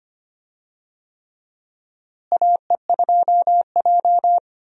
Morse code, heard as a keyed tone, about 25 wpm, 700 Hz. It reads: AE2J
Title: USA Call Signs - 4 Characters - Repeated 3x - 25wpm